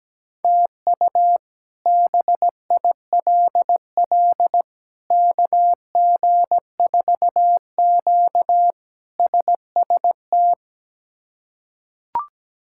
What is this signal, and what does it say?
Morse code, heard as a keyed tone, about 17 wpm, 700 Hz. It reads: TU BILL KG4Q SST
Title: K1USN SST QSO Practice - 17wpm